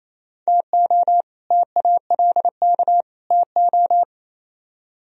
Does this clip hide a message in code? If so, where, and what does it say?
Yes: Morse code from 0.5 to 4.0 s, heard as a keyed tone, about 36 wpm, 700 Hz. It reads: TO TALK TO